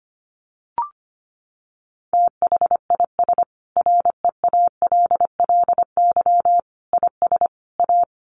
Morse code, heard as a keyed tone, about 25 wpm, 700 Hz. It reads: THIS REALLY IS A
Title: Sentences from Top 700 Words 25wpm